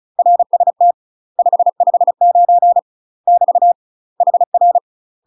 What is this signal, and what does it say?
Morse code, heard as a keyed tone, about 36 wpm, 700 Hz. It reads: RST 559 <BT> HR
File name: UK MASTER QSOs TRAINING/35WPM